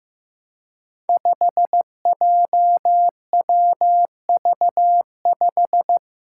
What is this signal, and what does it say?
Morse code, heard as a keyed tone, about 15 wpm, 700 Hz. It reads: 5JWV5
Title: Five Challenging Letter-Number Combinations 15wpm